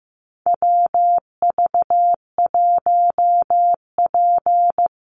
Morse code, heard as a keyed tone, about 15 wpm, 700 Hz. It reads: WV1P
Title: USA Call Signs 15wpm